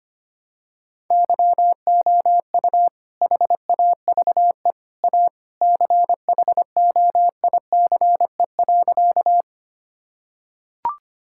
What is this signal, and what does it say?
Morse code, heard as a keyed tone, about 25 wpm, 700 Hz. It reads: YOU HAVE A CHOICE.
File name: practice / 25wpm / Sentences with 3-Word Phrases - 25wpm